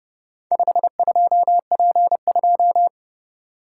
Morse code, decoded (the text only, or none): 52P2